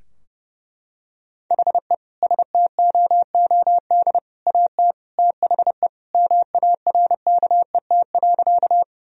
Morse code, heard as a keyed tone, about 30 wpm, 700 Hz. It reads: HE STOOD AT THE MARKET.
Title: Sentences from Top 1000 Words - Review Part 3 - 30wpm